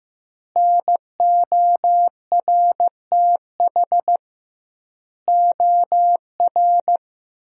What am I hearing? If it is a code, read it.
Morse code, heard as a keyed tone, about 15 wpm, 700 Hz. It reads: NORTH OR
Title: 2x Word-Spacing - Sentences from Top 400 Words - 15wpm